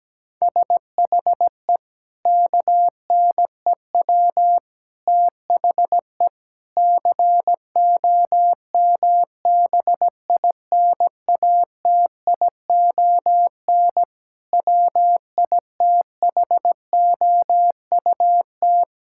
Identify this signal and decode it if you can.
Morse code, heard as a keyed tone, about 17 wpm, 700 Hz. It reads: SHE KNEW THE COMBINATION WITHOUT
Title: Sentences from Top 1000 Words - Review Part 6 - 17wpm